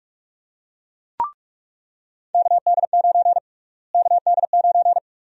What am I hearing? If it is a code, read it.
Morse code, heard as a keyed tone, about 41 wpm, 700 Hz. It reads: KD9 KD9
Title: USA Call Sign Prefixes - Repeated 2x - 45wpm